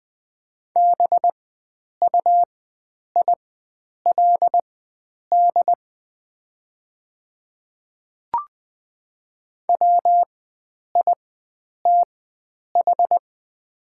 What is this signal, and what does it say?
Morse code, heard as a keyed tone, about 20 wpm, spaced as for 9 wpm, 700 Hz. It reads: BUILD WITH
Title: 4x Character Spacing - Top 300 Words - 20wpm